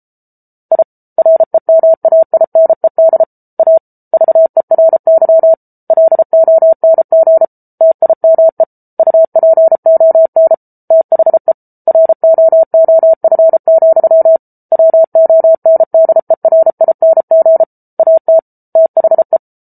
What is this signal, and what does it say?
Morse code, heard as a keyed tone, about 34 wpm, 650 Hz. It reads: I REMAINED A VERY LONG TIME UPON THE ROOF, WONDERING AT THE